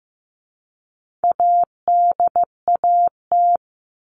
Morse code, heard as a keyed tone, about 15 wpm, 700 Hz. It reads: ADAT
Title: All Cut Numbers - 4-Digit 15wpm